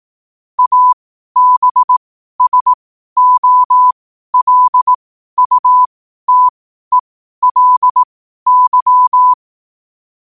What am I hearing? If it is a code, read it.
Morse code, heard as a keyed tone, about 18 wpm, spaced as for 12 wpm, 1000 Hz. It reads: ABSOLUTELY
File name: qotdfiles / 12wpm